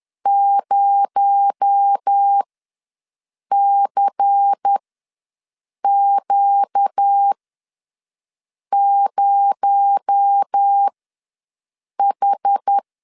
Morse code, heard as a keyed tone, about 10 wpm, spaced as for 5 wpm, 800 Hz. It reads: 0CQ0H
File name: LEZIONI / 6 PKX0 / 6 PKX0 ws 10wpm 4220